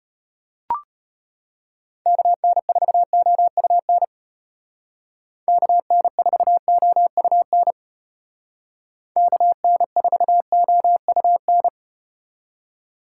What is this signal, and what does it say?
Morse code, heard as a keyed tone, about 34 wpm, 700 Hz. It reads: KN4OUN KN4OUN KN4OUN